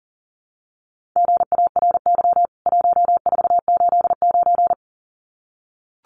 Morse code, heard as a keyed tone, about 40 wpm, 700 Hz. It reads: GARY 1489
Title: CWT Exchange 40wpm